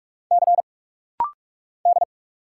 Morse code, heard as a keyed tone, about 45 wpm, 700 Hz. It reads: C D
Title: Single Letter-Number Alt - Rapid-Fire 45wpm